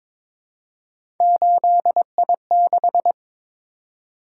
Morse code, heard as a keyed tone, about 21 wpm, 700 Hz. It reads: 8I6
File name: practice / 22wpm / Three Letter-Number Combinations - 22wpm